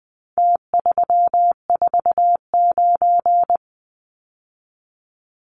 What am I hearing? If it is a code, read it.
Morse code, heard as a keyed tone, about 20 wpm, 700 Hz. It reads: T349
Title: Numbers - Contest Style - 4-Digit 20wpm